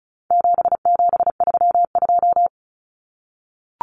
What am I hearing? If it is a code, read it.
Morse code, heard as a keyed tone, about 34 wpm, 700 Hz. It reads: 7732
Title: Common Cut Numbers - 4-Digit 35wpm